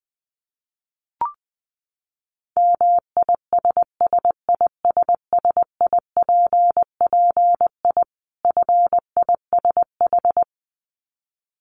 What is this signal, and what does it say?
Morse code, heard as a keyed tone, about 20 wpm, 700 Hz. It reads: MISSISSIPPI FISH
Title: Dit Challenge 20wpm